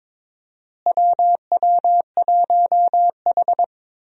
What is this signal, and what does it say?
Morse code, heard as a keyed tone, about 22 wpm, 700 Hz. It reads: WW1H
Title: USA Call Signs - 4 Characters - Repeated 3x - 22wpm